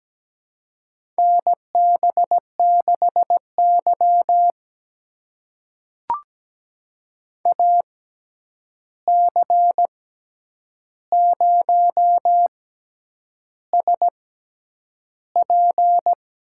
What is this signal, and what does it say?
Morse code, heard as a keyed tone, about 17 wpm, spaced as for 8 wpm, 700 Hz. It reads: NB6Y AC0SP
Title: USA Call Signs - 6x Character Spacing - 17wpm